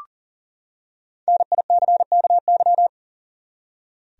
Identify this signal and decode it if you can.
Morse code, heard as a keyed tone, about 40 wpm, 700 Hz. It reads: NICKY
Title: USA Common Names - Max 5 Letters - 40wpm